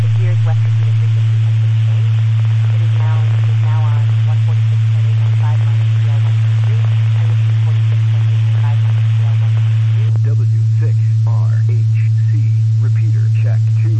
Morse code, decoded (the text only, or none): none